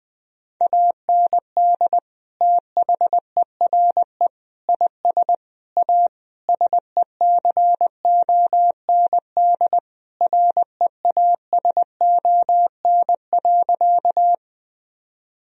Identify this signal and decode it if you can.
Morse code, heard as a keyed tone, about 20 wpm, 700 Hz. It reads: AND THERE IS A SECOND REASON.
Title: Sentences from Top 1000 Words - Review Part 1 - 20wpm